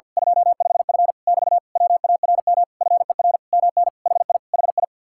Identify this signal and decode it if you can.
Morse code, heard as a keyed tone, about 50 wpm, 700 Hz. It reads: 25F <BT> WARM FER MN HI HI
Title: Speed-Racing Common QSO Phrases Encore - 50wpm